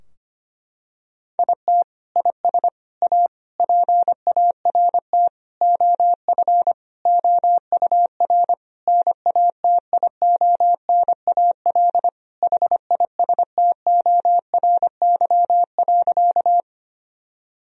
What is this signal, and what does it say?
Morse code, heard as a keyed tone, about 25 wpm, 700 Hz. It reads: IT IS A PART OF OUR NATIONAL HISTORY.
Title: Sentences from Top 1000 Words - Review Part 4 - 25wpm